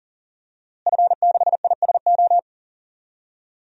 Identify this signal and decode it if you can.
Morse code, heard as a keyed tone, about 37 wpm, 700 Hz. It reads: F6ISO
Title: DXCC-100 Call Signs - 5 Characters - Repeated 3x - 40wpm